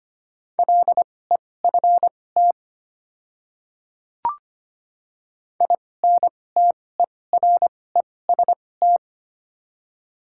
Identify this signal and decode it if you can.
Morse code, heard as a keyed tone, about 25 wpm, spaced as for 18 wpm, 700 Hz. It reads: LEFT INTEREST
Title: 2x Character Spacing - Top 500 Words - 25wpm